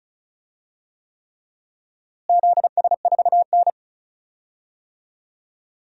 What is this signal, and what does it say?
Morse code, heard as a keyed tone, about 36 wpm, 700 Hz. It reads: ZS4N
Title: DXCC-100 Call Signs - 4 Characters - Repeated 3x - 35wpm